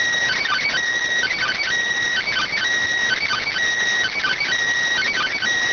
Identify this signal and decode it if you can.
Emergency Alert System (SAME) tones, heard as harsh buzzing bursts. It is heard from 1.2 to 5.4 s.